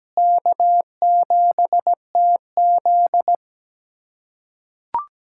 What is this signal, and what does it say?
Morse code, heard as a keyed tone, about 17 wpm, 700 Hz. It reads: K7TZ
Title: USA Callsign DE Callsign - 4 Characters - 17wpm